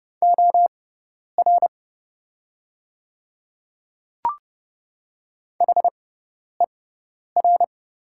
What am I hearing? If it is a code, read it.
Morse code, heard as a keyed tone, about 31 wpm, spaced as for 10 wpm, 700 Hz. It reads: OR HER
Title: 6x Character Spacing - Top 100 Words - 30wpm